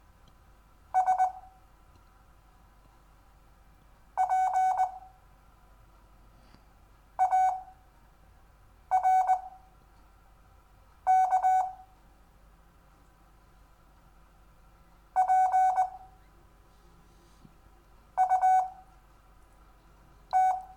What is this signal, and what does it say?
Morse code, heard as a keyed tone, about 20 wpm, spaced as for 4 wpm, 750 Hz. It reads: SPARK PUT
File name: cw / Lesson4 REVIEW